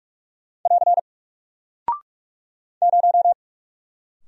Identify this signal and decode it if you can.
Morse code, heard as a keyed tone, about 44 wpm, 700 Hz. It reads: <AR> 0
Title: Single Character (Letter:Number:Punctuation:Prosign) 45wpm